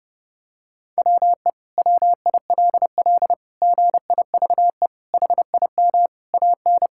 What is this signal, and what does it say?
Morse code, heard as a keyed tone, about 30 wpm, 700 Hz. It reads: WE WILL GIVE HIM AN
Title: Sentences from Top 1000 Words - Review Part 5 - 30wpm